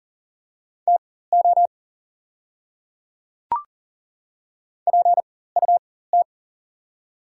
Morse code, heard as a keyed tone, about 40 wpm, spaced as for 19 wpm, 700 Hz. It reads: TO PUT
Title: 4x Character Spacing - Top 200 Words - 40wpm